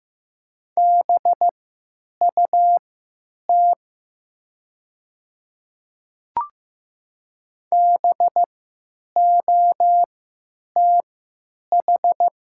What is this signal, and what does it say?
Morse code, heard as a keyed tone, about 15 wpm, spaced as for 9 wpm, 700 Hz. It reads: BUT BOTH
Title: 3x Character Spacing - Top 300 Words - 15wpm